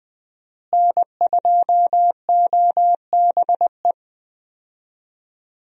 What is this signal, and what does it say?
Morse code, heard as a keyed tone, about 20 wpm, 700 Hz. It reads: N2OBE
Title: USA Call Signs - 5 Characters - Repeated 3x - 20wpm